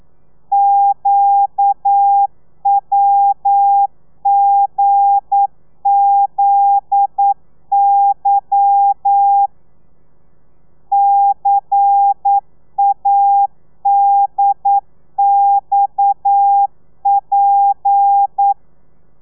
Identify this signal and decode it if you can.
Morse code, heard as a keyed tone, about 9 wpm, 800 Hz. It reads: QWGZY CADXP
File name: telegrafia / i2tfj / Lez11 40